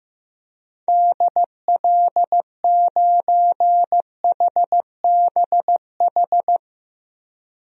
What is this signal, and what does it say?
Morse code, heard as a keyed tone, about 15 wpm, 700 Hz. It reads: DL9HBH
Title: DXCC-100 Call Signs - 6 Characters - 15wpm